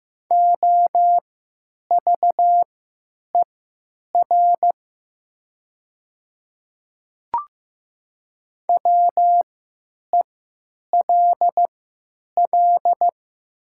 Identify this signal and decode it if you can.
Morse code, heard as a keyed tone, about 15 wpm, spaced as for 9 wpm, 700 Hz. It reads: OVER WELL
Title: 3x Character Spacing - Top 200 Words - 15wpm